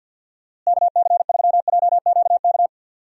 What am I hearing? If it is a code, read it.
Morse code, heard as a keyed tone, about 51 wpm, 700 Hz. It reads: KC3JQK
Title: USA Call Signs - 6 Characters - Repeated 3x - 50wpm